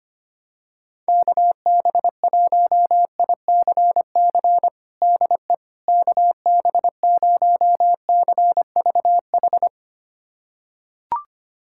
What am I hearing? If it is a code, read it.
Morse code, heard as a keyed tone, about 25 wpm, 700 Hz. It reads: KB1ICC DE KB0CVH
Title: USA Callsign DE Callsign 25wpm